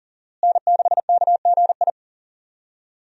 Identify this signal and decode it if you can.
Morse code, heard as a keyed tone, about 40 wpm, 700 Hz. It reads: N6KGI
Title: USA Call Signs - 5 Characters - Repeated 2x - 40wpm